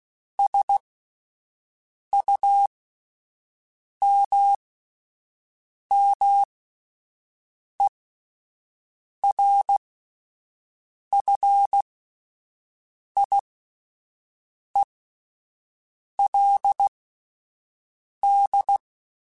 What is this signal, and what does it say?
Morse code, heard as a keyed tone, about 16 wpm, spaced as for 5 wpm, 800 Hz. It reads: SUMMERFIELD